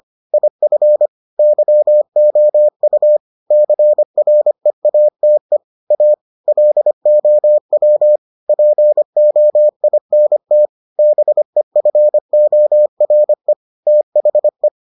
Morse code, heard as a keyed tone, about 25 wpm, 600 Hz. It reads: IF YOU CREATE A LOW POINT BEFORE THE